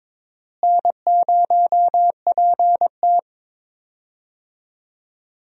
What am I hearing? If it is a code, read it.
Morse code, heard as a keyed tone, about 22 wpm, 700 Hz. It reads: N0PT
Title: USA Call Signs - 4 Characters - Repeated 3x - 22wpm